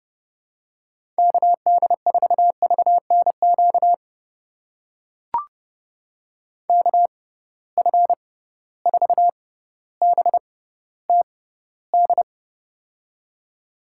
Morse code, heard as a keyed tone, about 30 wpm, spaced as for 15 wpm, 700 Hz. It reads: KD4VNQ KF4BTD
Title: USA Call Signs - 6 Characters - 6x Character Spacing - 30wpm